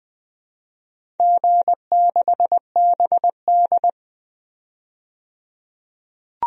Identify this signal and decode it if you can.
Morse code, heard as a keyed tone, about 20 wpm, 700 Hz. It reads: G6BD
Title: DXCC-100 Call Signs - 4 Characters - 20wpm